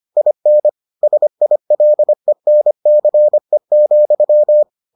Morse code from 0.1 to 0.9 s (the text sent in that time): IN